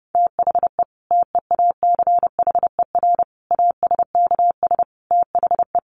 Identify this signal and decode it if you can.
Morse code, heard as a keyed tone, about 30 wpm, 700 Hz. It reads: THE TEACHER ASKS THE
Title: Additional Easy Phrases 30wpm